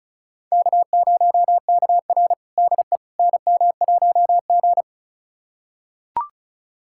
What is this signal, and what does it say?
Morse code, heard as a keyed tone, about 36 wpm, 700 Hz. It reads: K0KR DE NM1G
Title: USA Callsign DE Callsign - 4 Characters - 35wpm